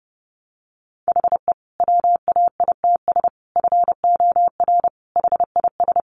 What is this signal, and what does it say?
Morse code, heard as a keyed tone, about 30 wpm, 700 Hz. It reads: HE WAITS FOR HIS
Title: Additional Easy Phrases 30wpm